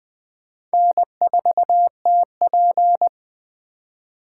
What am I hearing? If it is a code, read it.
Morse code, heard as a keyed tone, about 20 wpm, 700 Hz. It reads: N4TP